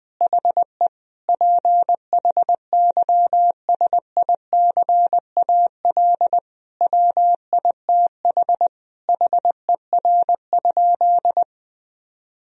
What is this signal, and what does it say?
Morse code, heard as a keyed tone, about 20 wpm, 700 Hz. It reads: HE PHYSICAL WITH HER?
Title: Sentences from Top 1000 Words - Review Part 4 - 20wpm